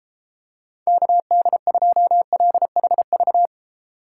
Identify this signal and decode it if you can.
Morse code, heard as a keyed tone, about 31 wpm, 700 Hz. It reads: KD2LHV